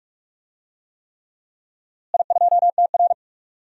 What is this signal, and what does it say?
Morse code, heard as a keyed tone, about 46 wpm, 700 Hz. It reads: I2TR
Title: DXCC-100 Call Signs - 4 Characters - Repeated 3x - 45wpm